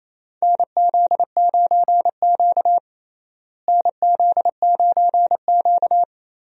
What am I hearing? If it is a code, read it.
Morse code, heard as a keyed tone, about 27 wpm, 700 Hz. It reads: NZ9Q NZ9Q
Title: USA Call Signs - 4 Characters - Repeated 3x - 28wpm